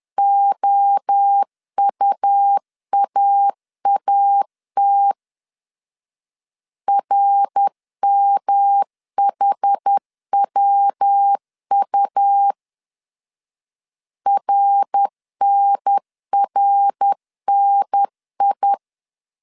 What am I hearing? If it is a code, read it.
Morse code, heard as a keyed tone, about 10 wpm, 800 Hz. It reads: OUAAT RMHWU RNRNI
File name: CW / DET / 1 3 1500C S B 10wpm 3335